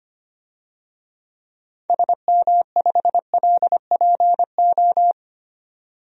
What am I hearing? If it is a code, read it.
Morse code, heard as a keyed tone, about 25 wpm, 690 Hz. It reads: SM5LPO
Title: DXCC-100 Call Signs - 6 Characters - 25wpm